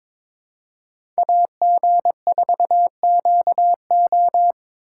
Morse code, heard as a keyed tone, about 22 wpm, 700 Hz. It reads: AG4QO
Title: USA Call Signs - 2x Character Spacing - 22wpm